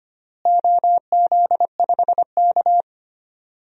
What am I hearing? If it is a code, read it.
Morse code, heard as a keyed tone, about 25 wpm, 700 Hz. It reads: OZ5K
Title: DXCC-100 Call Signs - 4 Characters - Repeated 2x - 25wpm